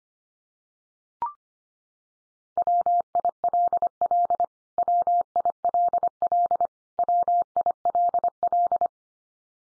Morse code, heard as a keyed tone, about 25 wpm, 700 Hz. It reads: WILL WILL WILL
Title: Top 100 Words 25wpm x3